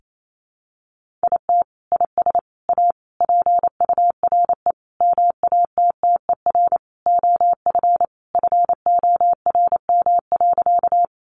Morse code, heard as a keyed tone, about 28 wpm, 700 Hz. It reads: IT IS A PURE MATTER OF FORM.
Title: Sentences from Top 1500 Words 28wpm